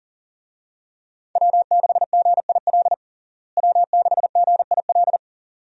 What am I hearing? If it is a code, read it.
Morse code, heard as a keyed tone, about 40 wpm, 700 Hz. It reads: W6GIL W6GIL
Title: USA Call Signs - 5 Characters - Repeated 2x - 40wpm